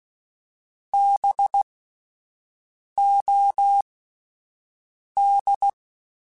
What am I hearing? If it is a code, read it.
Morse code, heard as a keyed tone, about 16 wpm, spaced as for 5 wpm, 800 Hz. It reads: BOD
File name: Spaced Alphas / Spaced Alphas mp3s / Bs sp @10